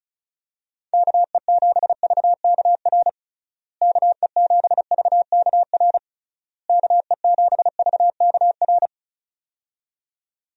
Morse code, heard as a keyed tone, about 34 wpm, 700 Hz. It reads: KE7VKR KE7VKR KE7VKR
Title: USA Call Signs - 6 Characters - Repeated 3x - 35wpm